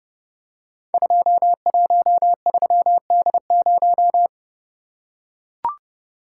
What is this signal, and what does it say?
Morse code, heard as a keyed tone, about 29 wpm, 700 Hz. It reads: 213D0
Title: Code Groups - 5 Letter:Numbers - 30wpm